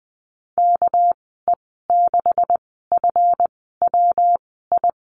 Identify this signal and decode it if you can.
Morse code, heard as a keyed tone, about 20 wpm, spaced as for 15 wpm, 700 Hz. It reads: KE6FWI
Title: 200 Lesson 32g - Letter k - Call Signs 20wpm